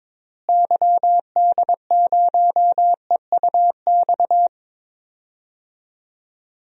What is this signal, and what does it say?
Morse code, heard as a keyed tone, about 22 wpm, 700 Hz. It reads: YD0EUX
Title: DXCC-100 Call Signs - 6 Characters - Repeated 3x - 22wpm